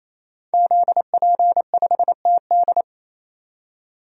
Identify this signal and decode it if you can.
Morse code, heard as a keyed tone, about 27 wpm, 700 Hz. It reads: ZP5TD
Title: DXCC-100 Call Signs - 5 Characters - 28wpm